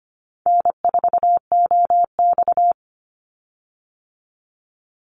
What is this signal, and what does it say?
Morse code, heard as a keyed tone, about 24 wpm, 700 Hz. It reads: N4OX
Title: Easy Call Signs Repeated 3x - 25wpm